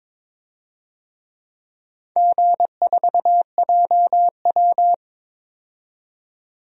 Morse code, heard as a keyed tone, about 21 wpm, 700 Hz. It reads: G4JW
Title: DXCC-100 Call Signs - 4 Characters - 22wpm